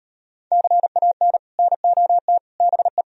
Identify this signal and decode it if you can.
Morse code, heard as a keyed tone, about 38 wpm, 700 Hz. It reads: CAN NOT BE